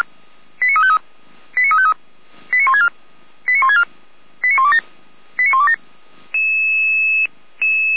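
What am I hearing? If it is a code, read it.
Five-tone selective call (ZVEI), heard as quick runs of five tones, tones of about 80 ms. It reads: 89353 89353 89164 89164 89127 89127
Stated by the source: ZVEI five-tone selective call received on SDR.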